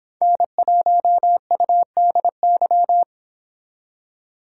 Morse code, heard as a keyed tone, about 29 wpm, 700 Hz. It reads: N1UDY